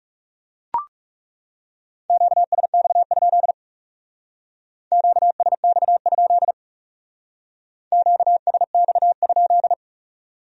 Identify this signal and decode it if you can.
Morse code, heard as a keyed tone, about 38 wpm, 700 Hz. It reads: QSX? QSX? QSX?